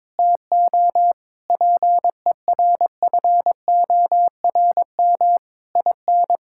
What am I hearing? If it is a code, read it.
Morse code, heard as a keyed tone, about 22 wpm, 700 Hz. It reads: TO PERFORM IN
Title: Sentences from Top 1000 Words - Review Part 5 - 22wpm